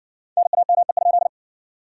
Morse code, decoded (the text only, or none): NAME?